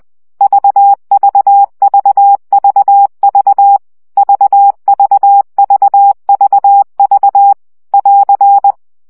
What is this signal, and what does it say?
Morse code, heard as a keyed tone, about 20 wpm, 800 Hz. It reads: VVVVV VVVVV <AR>
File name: CW gyakorlas v 17 baud